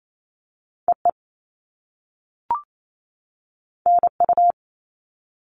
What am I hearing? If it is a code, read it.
Morse code, heard as a keyed tone, about 28 wpm, 700 Hz. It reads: EE NU
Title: All Cut Numbers - 2-Digit 28wpm